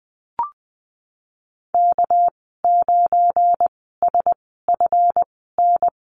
Morse code, heard as a keyed tone, about 20 wpm, spaced as for 14 wpm, 700 Hz. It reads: K9SFN
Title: 207 Lesson 33g - Letter j - Call Signs 20wpm